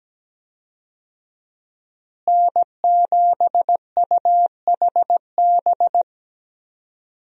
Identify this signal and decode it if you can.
Morse code, heard as a keyed tone, about 17 wpm, 700 Hz. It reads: N7UHB
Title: USA Call Signs - 5 Characters - Repeated 3x - 17wpm